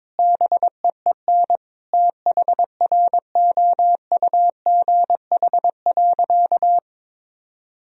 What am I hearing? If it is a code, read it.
Morse code, heard as a keyed tone, about 22 wpm, 700 Hz. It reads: BEEN THROUGH.
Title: Sentences from Top 1000 Words - Review Part 1 - 22wpm